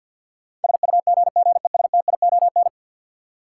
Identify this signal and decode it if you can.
Morse code, heard as a keyed tone, about 52 wpm, 700 Hz. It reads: SUGGESTION
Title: Sentences from Top 1600 Words - New Word Drill Alt - 50wpm